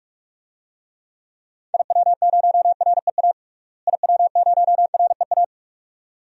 Morse code, heard as a keyed tone, about 46 wpm, 700 Hz. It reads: IW0REA IW0REA
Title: DXCC-100 Call Signs - 6 Characters - Repeated 2x - 45wpm